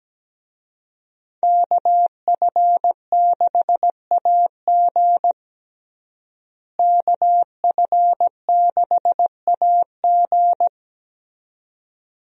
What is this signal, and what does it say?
Morse code, heard as a keyed tone, about 17 wpm, 700 Hz. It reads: KF6AG KF6AG
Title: USA Call Signs - 5 Characters - Repeated 2x - 17wpm